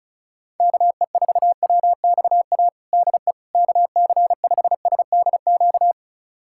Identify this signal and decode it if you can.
Morse code, heard as a keyed tone, about 34 wpm, 700 Hz. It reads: KE4WXA DE KC5SDQ
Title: USA Callsign DE Callsign - 6 Characters - 35wpm